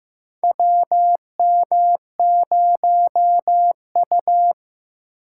Morse code, heard as a keyed tone, about 15 wpm, 700 Hz. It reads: WM0U
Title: USA Call Signs - 4 Characters - 6x Character Spacing - 15wpm